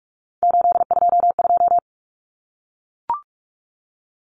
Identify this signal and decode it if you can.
Morse code, heard as a keyed tone, about 44 wpm, 700 Hz. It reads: 822
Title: Common Cut Numbers - 3-Digit 45wpm